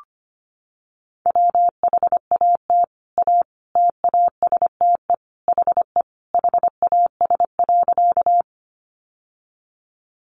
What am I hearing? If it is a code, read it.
Morse code, heard as a keyed tone, about 25 wpm, 700 Hz. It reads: WHAT A TASTE HE HAS.
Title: Sentences from Top 1100 Words 25wpm